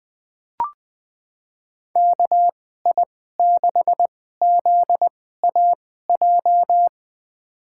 Morse code, heard as a keyed tone, about 20 wpm, spaced as for 15 wpm, 700 Hz. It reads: KI6ZAJ
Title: USA Call Signs - 6 Characters - 2x Character Spacing - 20wpm